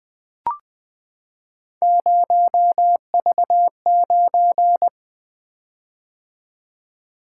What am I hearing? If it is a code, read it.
Morse code, heard as a keyed tone, about 20 wpm, 700 Hz. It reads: 0V9
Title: Three Letter-Number Combinations - 20wpm